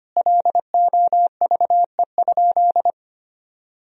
Morse code, heard as a keyed tone, about 25 wpm, 700 Hz. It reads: LOVE?
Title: Sentences from Top 1000 Words - Review Part 1 - 25wpm